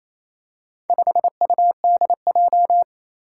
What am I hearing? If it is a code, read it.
Morse code, heard as a keyed tone, about 28 wpm, 700 Hz. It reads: 5UDJ
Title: Four Challenging Letter-Number Combinations 28wpm